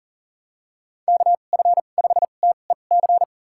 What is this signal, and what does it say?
Morse code, heard as a keyed tone, about 40 wpm, spaced as for 29 wpm, 700 Hz. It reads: KF5TEC
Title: USA Call Signs - 6 Characters - 2x Character Spacing - 40wpm